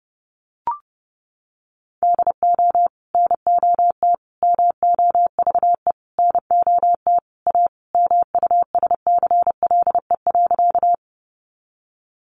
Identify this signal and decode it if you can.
Morse code, heard as a keyed tone, about 30 wpm, 700 Hz. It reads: DO NOT MOVE NOT A MUSCLE.
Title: Sentences from Top 1300 Words 30wpm